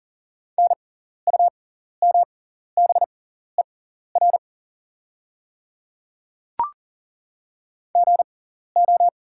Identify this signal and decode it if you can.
Morse code, heard as a keyed tone, about 40 wpm, spaced as for 14 wpm, 700 Hz. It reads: NUMBER GO
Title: 6x Character Spacing - Top 100 Words - 40wpm